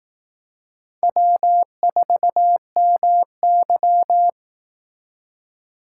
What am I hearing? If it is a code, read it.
Morse code, heard as a keyed tone, about 18 wpm, 700 Hz. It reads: W4MY
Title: Speed Racing - USA 1x2 Call Signs 15wpm